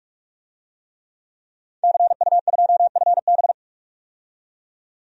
Morse code, heard as a keyed tone, about 46 wpm, 700 Hz. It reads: CU2FB
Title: DXCC-100 Call Signs - 5 Characters - 45wpm